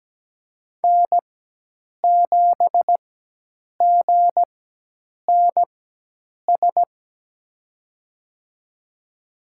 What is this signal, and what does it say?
Morse code, heard as a keyed tone, about 17 wpm, spaced as for 8 wpm, 700 Hz. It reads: N7GNS
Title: USA Call Signs - 5 Characters - 4x Character Spacing - 17wpm